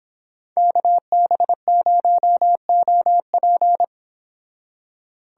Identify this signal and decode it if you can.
Morse code, heard as a keyed tone, about 25 wpm, 700 Hz. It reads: KB0OP